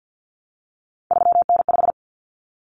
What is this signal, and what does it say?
Morse code, heard as a keyed tone, about 50 wpm, 700 Hz. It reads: 3N5
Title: Common Cut Numbers - 3-Digit 50wpm